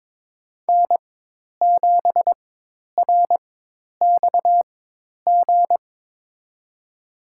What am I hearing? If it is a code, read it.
Morse code, heard as a keyed tone, about 22 wpm, spaced as for 10 wpm, 700 Hz. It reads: N7RXG
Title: USA Call Signs - 5 Characters - 4x Character Spacing - 22wpm